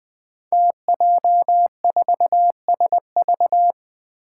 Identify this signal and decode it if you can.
Morse code, heard as a keyed tone, about 20 wpm, 700 Hz. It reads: TJ4SV
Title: Five Challenging Letter-Number Combinations 20wpm